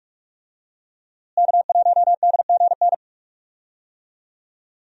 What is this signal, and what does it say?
Morse code, heard as a keyed tone, about 46 wpm, 700 Hz. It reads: K1DGN
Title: USA Call Signs - 5 Characters - Repeated 2x - 45wpm